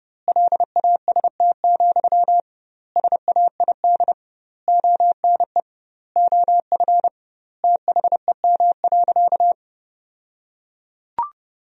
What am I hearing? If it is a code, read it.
Morse code, heard as a keyed tone, about 32 wpm, 700 Hz. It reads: LAST, SAID ONE OF THEM.
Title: 2x Word-Spacing - Sentences from Top 300 Words - 30wpm